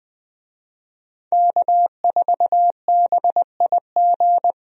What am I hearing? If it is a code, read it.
Morse code, heard as a keyed tone, about 20 wpm, 700 Hz. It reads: K4BIG
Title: USA Call Signs - 5 Characters - 20wpm